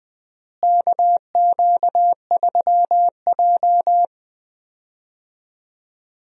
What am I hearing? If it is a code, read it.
Morse code, heard as a keyed tone, about 20 wpm, 700 Hz. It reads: KQ3J
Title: USA Call Signs - 4 Characters - Repeated 3x - 20wpm